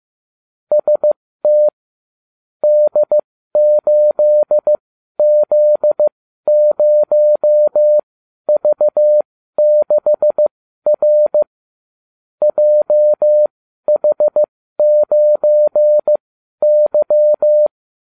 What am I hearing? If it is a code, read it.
Morse code, heard as a keyed tone, about 15 wpm, spaced as for 12 wpm, 600 Hz. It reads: ST D8Z0V6R JH9Y